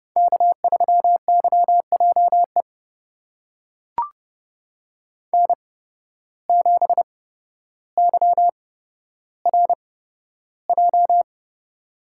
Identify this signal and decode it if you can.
Morse code, heard as a keyed tone, about 32 wpm, spaced as for 13 wpm, 700 Hz. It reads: K3YJE N7YRJ
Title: USA Call Signs - 5 Characters - 8x Character Spacing - 30wpm